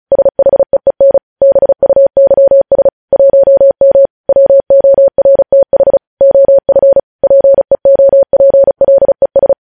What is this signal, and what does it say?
Morse code, heard as a keyed tone, about 36 wpm, 550 Hz. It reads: SHEEN BUYS 1M WORTH OF PEOPLES
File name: podcast / News headlines-1x-35x35